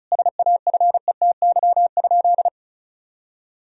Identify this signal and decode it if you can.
Morse code, heard as a keyed tone, about 34 wpm, 700 Hz. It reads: SAFETY?
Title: Sentences from Top 1000 Words - Review Part 4 - 35wpm